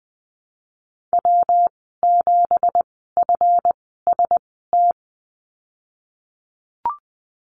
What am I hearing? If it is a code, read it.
Morse code, heard as a keyed tone, about 20 wpm, spaced as for 14 wpm, 700 Hz. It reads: W7FST
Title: 186 Lesson 30g - Letter b - Call Signs 20wpm